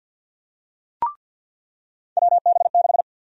Morse code, heard as a keyed tone, about 48 wpm, 700 Hz. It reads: WB6
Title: USA Call Sign Prefixes - 50wpm